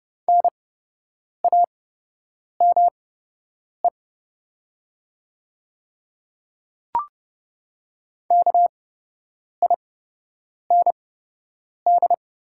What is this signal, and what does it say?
Morse code, heard as a keyed tone, about 30 wpm, spaced as for 8 wpm, 700 Hz. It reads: NAME KIND
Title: 8x Character Spacing - Top 200 Words - 30wpm